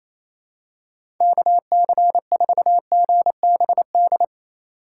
Morse code, heard as a keyed tone, about 27 wpm, 700 Hz. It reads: KC4GBD
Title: USA Call Signs - 2x Character Spacing - 28wpm